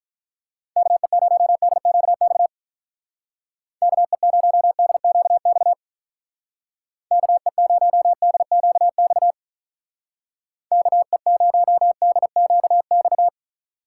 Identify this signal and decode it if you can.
Morse code, heard as a keyed tone, about 45 wpm, 700 Hz. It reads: KE0DQX KE0DQX KE0DQX KE0DQX